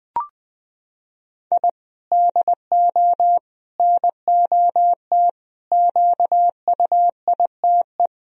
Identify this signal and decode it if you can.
Morse code, heard as a keyed tone, about 20 wpm, 700 Hz. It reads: I DO NOT QUITE
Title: Sentences from Top 1000 Words - Review Part 1 - 20wpm